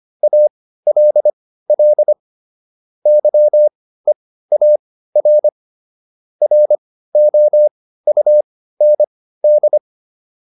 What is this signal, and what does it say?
Morse code, heard as a keyed tone, about 25 wpm, spaced as for 15 wpm, 600 Hz. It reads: ALL YEAR ROUND